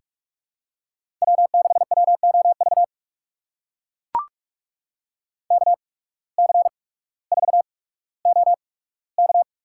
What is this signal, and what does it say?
Morse code, heard as a keyed tone, about 44 wpm, spaced as for 19 wpm, 700 Hz. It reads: W6WOV KC4OK
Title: USA Call Signs - 5 Characters - 8x Character Spacing - 45wpm